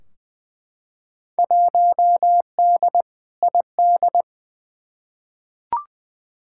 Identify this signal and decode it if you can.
Morse code, heard as a keyed tone, about 20 wpm, 700 Hz. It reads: 1D ID
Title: Field Day Exchanges - 20wpm